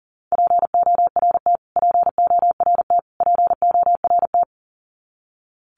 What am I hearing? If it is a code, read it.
Morse code, heard as a keyed tone, about 40 wpm, 700 Hz. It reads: PORT PORT PORT
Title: Top 500 Words Repeated 40wpm